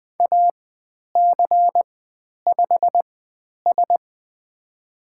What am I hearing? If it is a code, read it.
Morse code, heard as a keyed tone, about 20 wpm, spaced as for 10 wpm, 700 Hz. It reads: AC5S
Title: 095 Lesson 16g - Letter c - Call Signs 20wpm